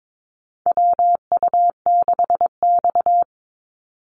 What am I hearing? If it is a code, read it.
Morse code, heard as a keyed tone, about 22 wpm, 700 Hz. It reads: WU6X
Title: Call Signs Repeated 2x - 22wpm